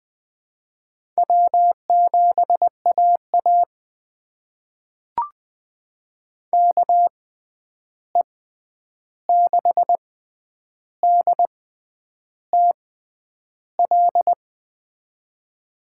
Morse code, heard as a keyed tone, about 20 wpm, spaced as for 9 wpm, 700 Hz. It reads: W7AA KE6DTL
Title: USA Call Signs - 6x Character Spacing - 20wpm